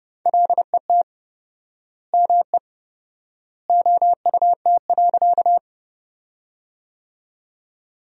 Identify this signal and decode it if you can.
Morse code, heard as a keyed tone, about 31 wpm, 700 Hz. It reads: LET ME OUT.
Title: 4x Word-Spacing - Sentences from Top 300 Words - 30wpm